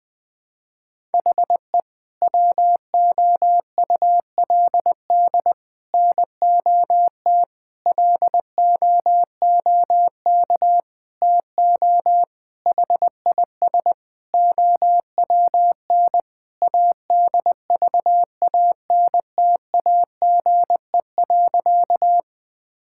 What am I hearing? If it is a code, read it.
Morse code, heard as a keyed tone, about 20 wpm, 700 Hz. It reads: HE WOULD NOT LOOK TO HIS OWN ADVANTAGE.
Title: Sentences from Top 1000 Words - Review Part 6 - 20wpm